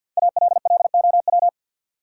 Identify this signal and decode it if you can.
Morse code, heard as a keyed tone, about 53 wpm, 700 Hz. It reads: ALLOW
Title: Sentences from Top 700 Words - New Word Drill - 50wpm